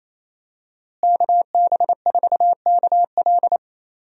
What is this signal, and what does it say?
Morse code, heard as a keyed tone, about 28 wpm, 700 Hz. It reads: KB4KL
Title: USA Call Signs - 5 Characters - Repeated 3x - 28wpm